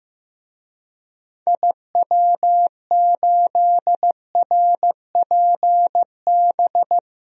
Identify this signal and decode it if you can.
Morse code, heard as a keyed tone, about 15 wpm, 700 Hz. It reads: IW8RPB
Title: DXCC-100 Call Signs - 6 Characters - 15wpm